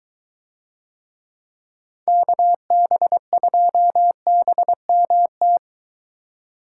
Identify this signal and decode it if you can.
Morse code, heard as a keyed tone, about 23 wpm, 700 Hz. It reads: KB2BMT